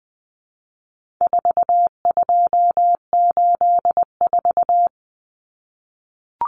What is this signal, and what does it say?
Morse code, heard as a keyed tone, about 20 wpm, 700 Hz. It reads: 4284
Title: Common Cut Numbers - 4-Digit 20wpm